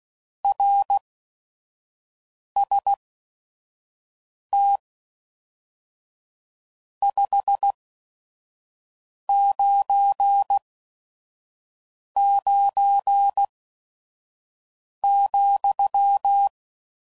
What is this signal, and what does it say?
Morse code, heard as a keyed tone, about 16 wpm, spaced as for 5 wpm, 800 Hz. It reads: RST 599,
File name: CW Code Groups @5WPM / RST's / Your RST's